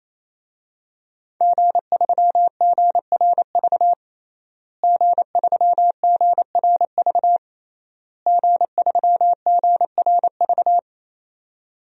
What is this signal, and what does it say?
Morse code, heard as a keyed tone, about 28 wpm, 700 Hz. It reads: G3GRV G3GRV G3GRV
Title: DXCC-100 Call Signs - 5 Characters - Repeated 3x - 28wpm